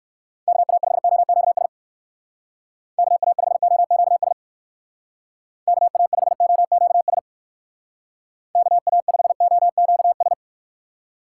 Morse code, heard as a keyed tone, about 55 wpm, 700 Hz. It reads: KA5OQS KA5OQS KA5OQS KA5OQS
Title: Speed Racing - USA 2x3 Call Signs 45wpm